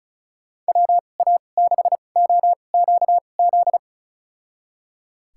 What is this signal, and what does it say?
Morse code, heard as a keyed tone, about 36 wpm, spaced as for 25 wpm, 700 Hz. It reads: WA6OQZ
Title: USA Call Signs - 6 Characters - 2x Character Spacing - 35wpm